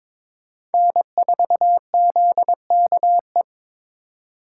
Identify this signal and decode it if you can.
Morse code, heard as a keyed tone, about 21 wpm, 700 Hz. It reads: N4ZKE
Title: USA Call Signs - 5 Characters - 4x Character Spacing - 22wpm